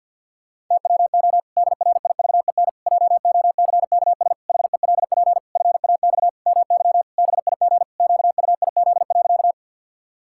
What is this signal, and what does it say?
Morse code, heard as a keyed tone, about 50 wpm, 700 Hz. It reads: TWO DRIVEN JOCKS HELP FAX MY BIG QUIZ.
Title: Pangram Challenge v2 - 50wpm